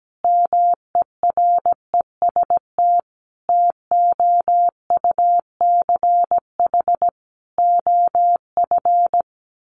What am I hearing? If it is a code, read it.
Morse code, heard as a keyed tone, about 17 wpm, 700 Hz. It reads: MEREST TOUCH OF
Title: A Sound Of Thunder 17wpm